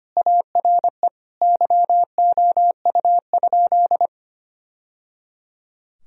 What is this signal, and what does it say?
Morse code, heard as a keyed tone, about 25 wpm, 700 Hz. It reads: ARE YOU?
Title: Sentences from Top 1000 Words - Review Part 1 - 25wpm